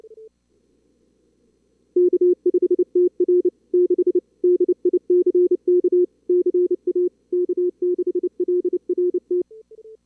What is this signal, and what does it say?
Morse code, heard as a keyed tone, about 28 wpm, 360 Hz. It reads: K5TR 6 DICK CA K6LRT
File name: CWSprint / Feb2022 / k5tr-feb-na-sprint-cw-k5ot -2300